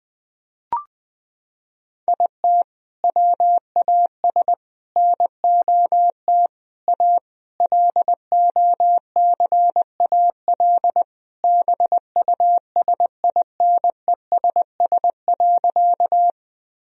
Morse code, heard as a keyed tone, about 20 wpm, 700 Hz. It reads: IT WAS NOT A LOCAL BUSINESS.
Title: Sentences from Top 1000 Words - Review Part 3 - 20wpm